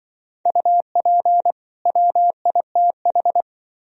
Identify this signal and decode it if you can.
Morse code, heard as a keyed tone, about 25 wpm, 700 Hz. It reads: UP WITH